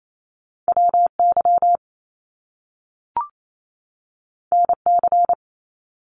Morse code, heard as a keyed tone, about 28 wpm, 700 Hz. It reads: WY NC
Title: US States 28wpm